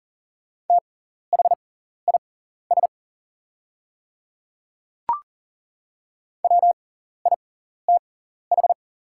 Morse code, heard as a keyed tone, about 40 wpm, spaced as for 14 wpm, 700 Hz. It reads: THIS WITH
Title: 6x Character Spacing - Top 100 Words - 40wpm